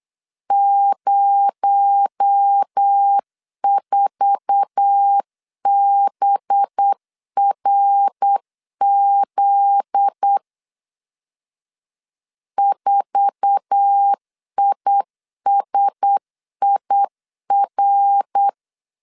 Morse code, heard as a keyed tone, about 8 wpm, 800 Hz. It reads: 04BRZ 4ISIR
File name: LEZIONI / 8 YJ45 / 8 YJ4 8wpm 3246